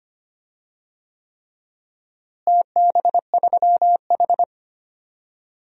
Morse code, heard as a keyed tone, about 25 wpm, 700 Hz. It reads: TB3H
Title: DXCC-100 Call Signs - 4 Characters - Repeated 3x - 25wpm